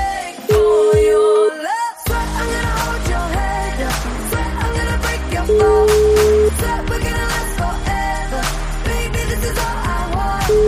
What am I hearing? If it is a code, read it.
Ringback tone (Europe, 425 Hz), with noise.